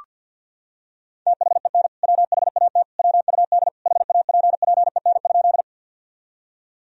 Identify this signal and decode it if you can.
Morse code, heard as a keyed tone, about 47 wpm, 700 Hz. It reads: THEN WHAT WUD HAPPEN?
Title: Sentences from Top 400 Words and Abbreviations 50wpm